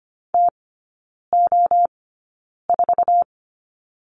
Morse code, heard as a keyed tone, about 25 wpm, spaced as for 9 wpm, 690 Hz. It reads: TO4
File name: Lesson 15e - Number 5 - Three Characters 25wpm